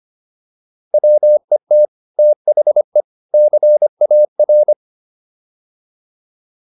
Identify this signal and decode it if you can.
Morse code, heard as a keyed tone, about 25 wpm, 600 Hz. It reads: WET THE CAR